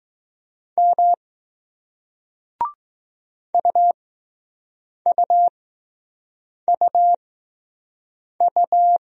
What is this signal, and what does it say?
Morse code, heard as a keyed tone, about 20 wpm, spaced as for 6 wpm, 700 Hz. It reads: M UUUU